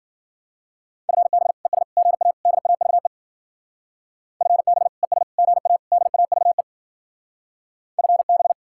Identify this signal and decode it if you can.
Morse code, heard as a keyed tone, about 53 wpm, 700 Hz. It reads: FB ES GA DAVE FB ES GA DAVE FB